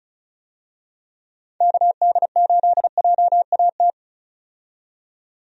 Morse code, heard as a keyed tone, about 34 wpm, 700 Hz. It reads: KD8JAT
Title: USA Call Signs - 6 Characters - 35wpm